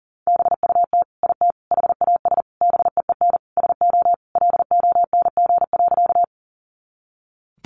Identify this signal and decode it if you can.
Morse code, heard as a keyed tone, about 40 wpm, 700 Hz. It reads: BUT IT HAS BEEN SO LONG.
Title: Sentences from Top 100 Words 40wpm